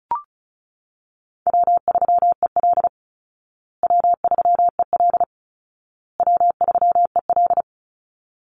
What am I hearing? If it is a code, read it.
Morse code, heard as a keyed tone, about 34 wpm, 700 Hz. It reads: W3EL W3EL W3EL
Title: Easy Call Signs Repeated 3x - 35wpm